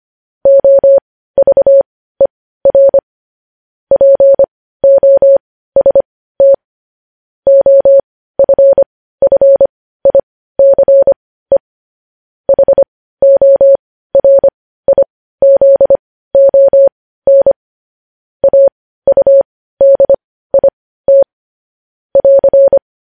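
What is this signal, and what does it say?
Morse code, heard as a keyed tone, about 25 wpm, spaced as for 15 wpm, 550 Hz. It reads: OVER POST OFFICE HORIZON AUDIT <AR>
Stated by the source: News headlines sent in Morse code at 15 wpm.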